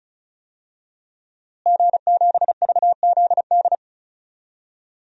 Morse code, heard as a keyed tone, about 36 wpm, 700 Hz. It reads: G7VZD
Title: DXCC-100 Call Signs - 5 Characters - Repeated 3x - 35wpm